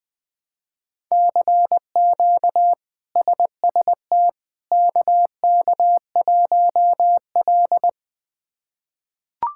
Morse code, heard as a keyed tone, about 20 wpm, 700 Hz. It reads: CQ SST KK1L
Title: K1USN SST QSO Practice - 20wpm